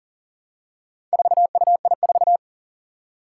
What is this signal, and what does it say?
Morse code, heard as a keyed tone, about 40 wpm, 700 Hz. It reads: 4UI4
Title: Four Challenging Letter-Number Combinations 40wpm